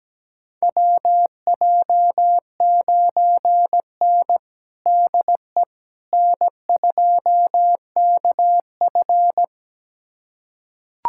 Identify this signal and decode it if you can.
Morse code, heard as a keyed tone, about 17 wpm, 700 Hz. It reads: WJ9N DE N2KF
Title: USA Callsign DE Callsign - 4 Characters - 17wpm